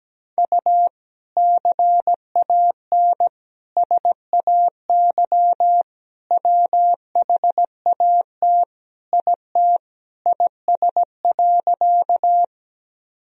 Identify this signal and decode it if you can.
Morse code, heard as a keyed tone, about 17 wpm, 700 Hz. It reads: U CAN SAY WHAT IT IS.
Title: Sentences from Top 200 Words and Abbreviations 17wpm